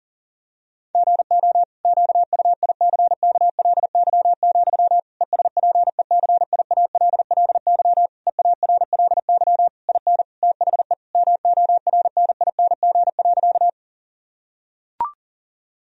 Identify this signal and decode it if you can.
Morse code, heard as a keyed tone, about 40 wpm, 700 Hz. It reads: GO QUICKLY, ESPECIALLY EARLY IN THE MORNING.
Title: Sentences from Top 1000 Words - Review Part 4 - 40wpm